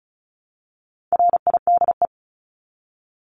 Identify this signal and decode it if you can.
Morse code, heard as a keyed tone, about 34 wpm, 700 Hz. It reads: RIDE
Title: Top 1001-1250 Words 35wpm